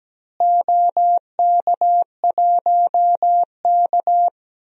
Morse code, heard as a keyed tone, about 17 wpm, 700 Hz. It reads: OK1K
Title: DXCC-100 Call Signs - 4 Characters - Repeated 3x - 17wpm